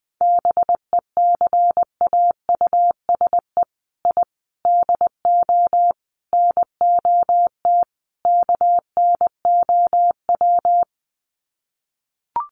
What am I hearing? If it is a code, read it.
Morse code, heard as a keyed tone, about 19 wpm, 700 Hz. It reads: BECAUSE I DO NOT KNOW
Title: Sets of 5 Words Encore2 20wpm